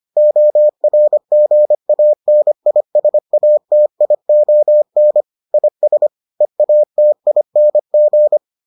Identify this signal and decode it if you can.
Morse code, heard as a keyed tone, about 25 wpm, 600 Hz. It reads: ORGANISATION IS EATING